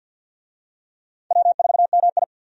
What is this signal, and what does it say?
Morse code, heard as a keyed tone, about 51 wpm, 700 Hz. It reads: W4MI
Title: USA Call Signs - 4 Characters - 50wpm